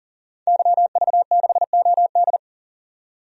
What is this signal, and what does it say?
Morse code, heard as a keyed tone, about 37 wpm, 700 Hz. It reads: YV6OD
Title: DXCC-100 Call Signs - 5 Characters - 40wpm